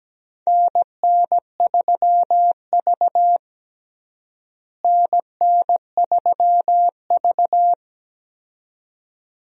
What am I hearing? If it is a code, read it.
Morse code, heard as a keyed tone, about 17 wpm, 700 Hz. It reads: NN3V NN3V
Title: USA Call Signs - 4 Characters - Repeated 2x - 17wpm